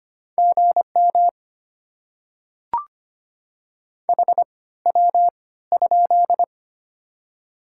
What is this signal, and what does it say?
Morse code, heard as a keyed tone, about 25 wpm, spaced as for 16 wpm, 700 Hz. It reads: GM HW?
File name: Top 20 QSO Elements - 3x Character Spacing - 25wpm